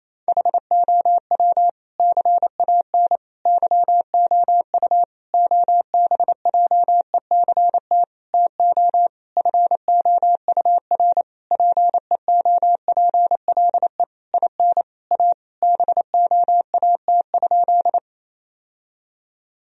Morse code, heard as a keyed tone, about 28 wpm, 700 Hz. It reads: HOW CAN YOU OBJECT TO FOUR PEOPLE IN A BOAT?
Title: Sentences from Top 1000 Words - Review Part 3 - 28wpm